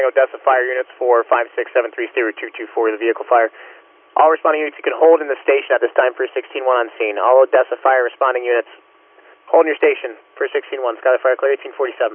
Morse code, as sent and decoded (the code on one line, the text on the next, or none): none